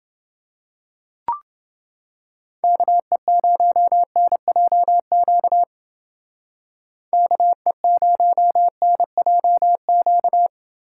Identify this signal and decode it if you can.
Morse code, heard as a keyed tone, about 29 wpm, 700 Hz. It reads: KE0NJQ KE0NJQ